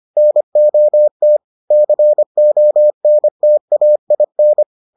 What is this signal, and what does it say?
Morse code, heard as a keyed tone, about 25 wpm, 600 Hz. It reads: NOT CONTAIN